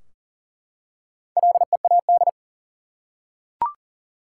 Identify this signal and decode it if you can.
Morse code, heard as a keyed tone, about 40 wpm, 700 Hz. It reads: LEAD
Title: Four Letter Words - 40wpm